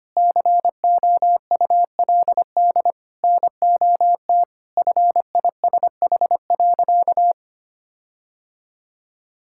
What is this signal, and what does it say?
Morse code, heard as a keyed tone, about 25 wpm, 700 Hz. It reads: COULD NOT FISH.
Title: Sentences from Top 1600 Words 25wpm